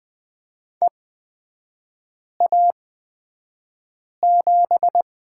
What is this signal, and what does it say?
Morse code, heard as a keyed tone, about 20 wpm, spaced as for 5 wpm, 700 Hz. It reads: EA7
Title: 243 Lesson 39e - Letter z - Three Characters 20wpm